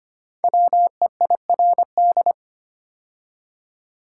Morse code, heard as a keyed tone, about 25 wpm, 700 Hz. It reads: WEIRD
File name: 5 Letter Words - First 3 Letters Given - 25wpm